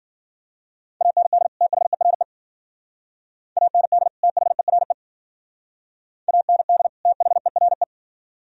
Morse code, heard as a keyed tone, about 49 wpm, 700 Hz. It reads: AND THERE AND THERE AND THERE
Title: Speed-Racing - Sets of 2 Words 40wpm